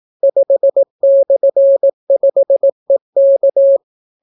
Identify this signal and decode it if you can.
Morse code, heard as a keyed tone, about 18 wpm, 550 Hz. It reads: 5/5EK